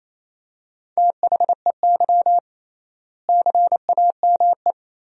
Morse code, heard as a keyed tone, about 28 wpm, 700 Hz. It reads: THEY CAME
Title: 3x Word-Spacing - Sentences from Top 300 Words - 28wpm